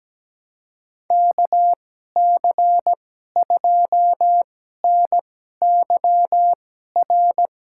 Morse code, heard as a keyed tone, about 17 wpm, spaced as for 12 wpm, 700 Hz. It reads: KC2NYR
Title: USA Call Signs - 6 Characters - 2x Character Spacing - 17wpm